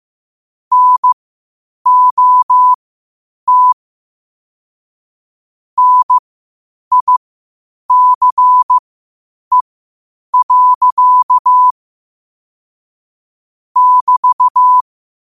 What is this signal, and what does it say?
Morse code, heard as a keyed tone, about 15 wpm, spaced as for 8 wpm, 1000 Hz. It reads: NOT NICE. <BT>
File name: qotdfiles / 8wpm